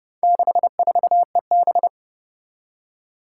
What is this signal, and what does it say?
Morse code, heard as a keyed tone, about 30 wpm, 700 Hz. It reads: 64EB
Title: Four Challenging Letter-Number Combinations 30wpm